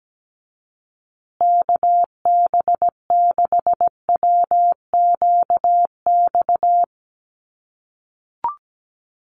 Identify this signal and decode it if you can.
Morse code, heard as a keyed tone, about 17 wpm, 700 Hz. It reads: KB6WQX
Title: USA Call Signs Repeated 3x 17wpm